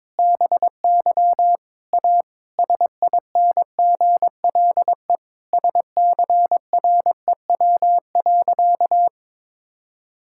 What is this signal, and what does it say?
Morse code, heard as a keyed tone, about 22 wpm, 700 Hz. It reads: BY A SINGLE SCREW.
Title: Sentences from Top 1600 Words 22wpm